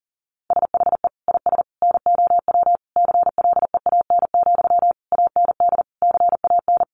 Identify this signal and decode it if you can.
Morse code, heard as a keyed tone, about 40 wpm, 700 Hz. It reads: SHE IS NOW CLEAN, AND CAN
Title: Sentences from Top 1100 Words 40wpm